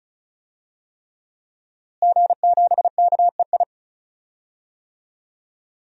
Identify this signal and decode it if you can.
Morse code, heard as a keyed tone, about 36 wpm, 700 Hz. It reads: G7KEI
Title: DXCC-100 Call Signs - 5 Characters - Repeated 3x - 35wpm